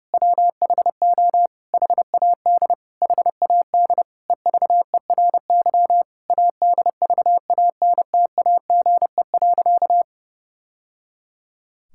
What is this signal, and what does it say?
Morse code, heard as a keyed tone, about 30 wpm, 700 Hz. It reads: WHO HAD HAD EVERY ADVANTAGE.
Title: Sentences from Top 1000 Words - Review Part 6 - 30wpm